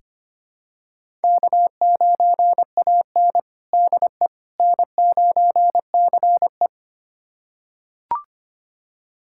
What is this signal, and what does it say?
Morse code, heard as a keyed tone, about 25 wpm, 700 Hz. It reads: K9AN DE N9CE
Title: USA Callsign DE Callsign - 4 Characters - 25wpm